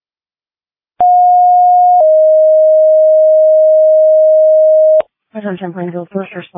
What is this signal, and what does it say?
Two-tone page, heard as a one-second tone, then a three-second tone, about 720 then 620 Hz, with speech. It is heard from 1.0 to 5.0 s.